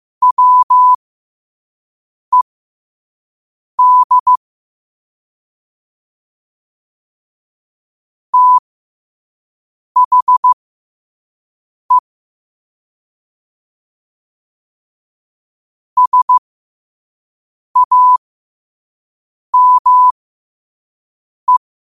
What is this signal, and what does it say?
Morse code, heard as a keyed tone, about 15 wpm, spaced as for 5 wpm, 1000 Hz. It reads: WED THE SAME
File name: qotdfiles / 5wpm